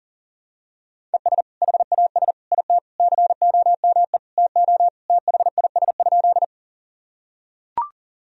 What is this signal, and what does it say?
Morse code, heard as a keyed tone, about 40 wpm, 700 Hz. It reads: ES HAS IT COME TO THIS?
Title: Sentences from Top 100 Words and Abbreviations 40wpm